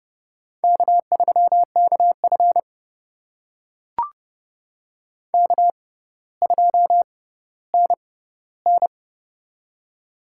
Morse code, heard as a keyed tone, about 30 wpm, spaced as for 15 wpm, 700 Hz. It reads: K3KF K2NN
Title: USA Call Signs - 4 Characters - 6x Character Spacing - 30wpm